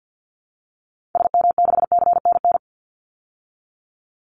Morse code, heard as a keyed tone, about 49 wpm, 700 Hz. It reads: SM6CNN
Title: Call Signs Repeated 3x - 50wpm